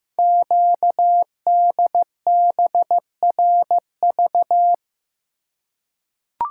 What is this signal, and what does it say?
Morse code, heard as a keyed tone, about 15 wpm, 700 Hz. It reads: QDBRV
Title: Code Groups - 5 Letters - 15wpm